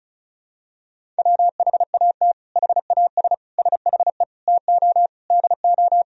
Morse code, heard as a keyed tone, about 34 wpm, 700 Hz. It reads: WHAT HAS SHE TO DO
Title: Sentences from Top 100 Words and Abbreviations 35wpm